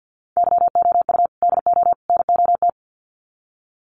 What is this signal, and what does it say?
Morse code, heard as a keyed tone, about 50 wpm, 700 Hz. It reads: YOU DO NOT
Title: Sets of 3 Words 50wpm